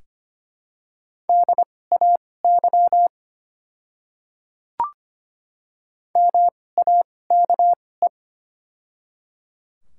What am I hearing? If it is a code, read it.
Morse code, heard as a keyed tone, about 25 wpm, spaced as for 18 wpm, 700 Hz. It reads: DAY MAKE
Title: 2x Character Spacing - Top 100 Words - 25wpm